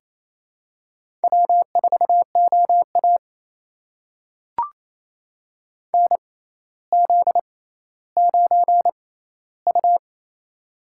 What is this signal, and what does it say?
Morse code, heard as a keyed tone, about 28 wpm, spaced as for 14 wpm, 700 Hz. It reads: W4OA NZ9U
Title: USA Call Signs - 4 Characters - 6x Character Spacing - 28wpm